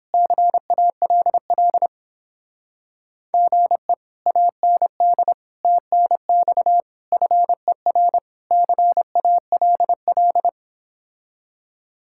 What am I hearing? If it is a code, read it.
Morse code, heard as a keyed tone, about 29 wpm, 700 Hz. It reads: CALL GE AND TNX FER CALL